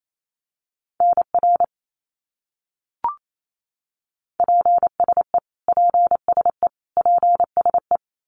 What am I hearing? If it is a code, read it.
Morse code, heard as a keyed tone, about 28 wpm, 700 Hz. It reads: NR PSE PSE PSE
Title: Abbreviations Repeated 3x - 28wpm